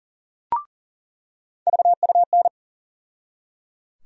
Morse code, heard as a keyed tone, about 39 wpm, 700 Hz. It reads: VUN
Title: USA Call Sign Postfixes - 3 Letters 40wpm